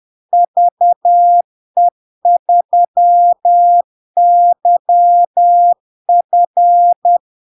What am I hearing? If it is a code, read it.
Morse code, heard as a keyed tone, about 10 wpm, 700 Hz. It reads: VE3YF
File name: LARK-FD-2022-Log-in-CW-10wpm-700-Hz